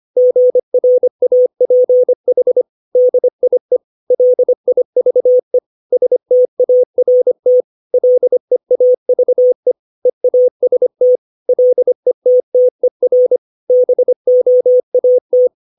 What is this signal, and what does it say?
Morse code, heard as a keyed tone, about 25 wpm, 500 Hz. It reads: GRAPH DIE LIVE START LEAVE EAST LETTER BOAT
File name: 632 words / 632 Words 25WPM